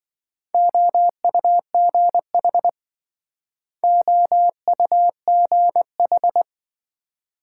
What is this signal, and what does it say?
Morse code, heard as a keyed tone, about 27 wpm, 700 Hz. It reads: OUGH OUGH